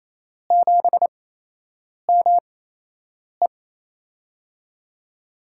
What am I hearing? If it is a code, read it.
Morse code, heard as a keyed tone, about 28 wpm, spaced as for 8 wpm, 700 Hz. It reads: 7ME
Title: USA Call Signs - 4 Characters - 8x Character Spacing - 28wpm